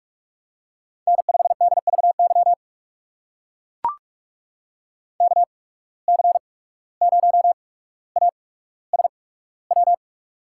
Morse code, heard as a keyed tone, about 44 wpm, spaced as for 18 wpm, 700 Hz. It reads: N5DVY KC0ASW
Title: USA Call Signs - 8x Character Spacing - 45wpm